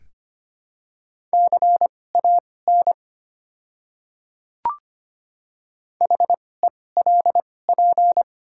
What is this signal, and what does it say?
Morse code, heard as a keyed tone, about 25 wpm, spaced as for 18 wpm, 700 Hz. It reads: CAN HELP
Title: 2x Character Spacing - Top 200 Words - 25wpm